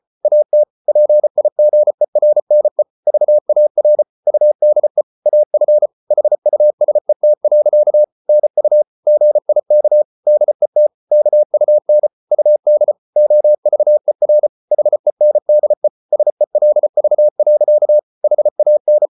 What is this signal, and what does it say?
Morse code, heard as a keyed tone, about 34 wpm, 600 Hz. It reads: AT PIGERNE VAR UDE AF HUSET. NU GIK DET KUN UD OVER HENDE SELV. HAN
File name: Ferie QRQ 34wpm